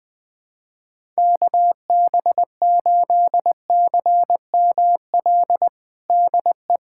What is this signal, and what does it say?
Morse code, heard as a keyed tone, about 20 wpm, 700 Hz. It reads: KB8CML DE
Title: USA Callsign DE Callsign 20wpm